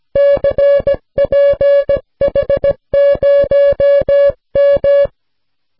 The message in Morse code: CPH0M